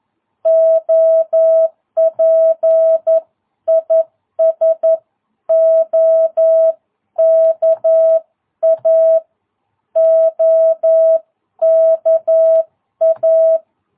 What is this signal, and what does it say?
Morse code, heard as a keyed tone, about 11 wpm, 660 Hz. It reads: OP IS OKA OKA
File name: jh8cbh / 2022cw